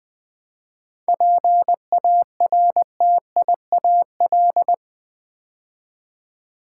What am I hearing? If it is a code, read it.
Morse code, heard as a keyed tone, about 20 wpm, 700 Hz. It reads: PARTIAL
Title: 7 Letter Words - First 3 Letters Given - 20wpm